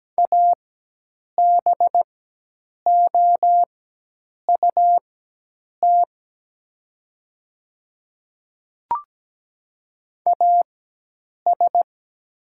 Morse code, heard as a keyed tone, about 17 wpm, spaced as for 8 wpm, 700 Hz. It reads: ABOUT AS
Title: 4x Character Spacing - Top 200 Words - 17wpm